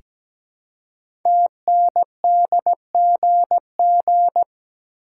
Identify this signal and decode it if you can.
Morse code, heard as a keyed tone, about 17 wpm, 700 Hz. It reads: TNDGG
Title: Code Groups - 5 Letters - 17wpm